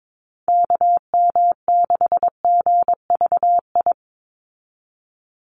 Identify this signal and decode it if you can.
Morse code, heard as a keyed tone, about 22 wpm, 700 Hz. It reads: KM6GVI
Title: USA Call Signs 22wpm